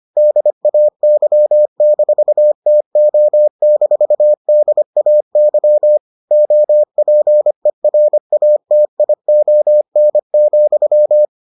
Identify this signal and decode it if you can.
Morse code, heard as a keyed tone, about 25 wpm, 600 Hz. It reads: DAY-TO-DAY OPERATION,